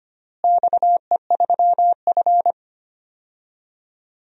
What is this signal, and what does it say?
Morse code, heard as a keyed tone, about 25 wpm, 700 Hz. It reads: XE3F
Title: DXCC-100 Call Signs - 4 Characters - Repeated 2x - 25wpm